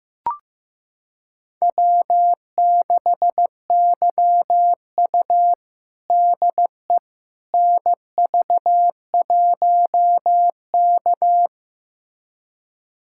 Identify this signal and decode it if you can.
Morse code, heard as a keyed tone, about 15 wpm, 700 Hz. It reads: W6YU DE NV1K
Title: USA Callsign DE Callsign - 4 Characters - 15wpm